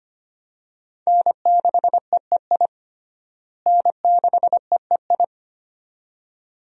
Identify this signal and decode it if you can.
Morse code, heard as a keyed tone, about 25 wpm, 700 Hz. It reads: N6EEI N6EEI
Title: USA Call Signs - 5 Characters - Repeated 2x - 25wpm